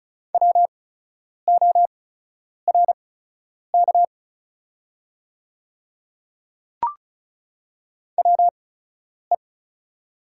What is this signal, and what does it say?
Morse code, heard as a keyed tone, about 36 wpm, spaced as for 10 wpm, 700 Hz. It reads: WORK WE
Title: 8x Character Spacing - Top 300 Words - 35wpm